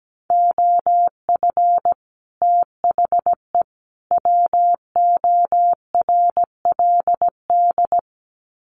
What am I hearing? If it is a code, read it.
Morse code, heard as a keyed tone, about 17 wpm, 700 Hz. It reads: OF THE WORLD
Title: Sets of 3 Words 17wpm